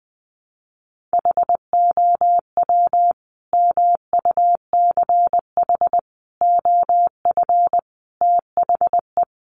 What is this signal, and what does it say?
Morse code, heard as a keyed tone, about 19 wpm, 700 Hz. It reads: HOW MUCH OF THE
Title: Sentences from Top 1200 Words 20wpm